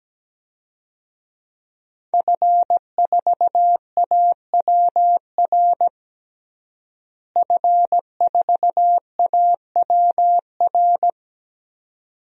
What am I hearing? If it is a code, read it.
Morse code, heard as a keyed tone, about 17 wpm, 700 Hz. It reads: F4AWR F4AWR
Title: DXCC-100 Call Signs - 5 Characters - Repeated 2x - 17wpm